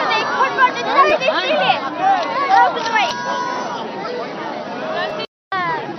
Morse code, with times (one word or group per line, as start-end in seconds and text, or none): none